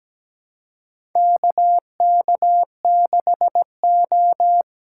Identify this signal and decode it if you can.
Morse code, heard as a keyed tone, about 17 wpm, 700 Hz. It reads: KK6O
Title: USA Call Signs - 6x Character Spacing - 17wpm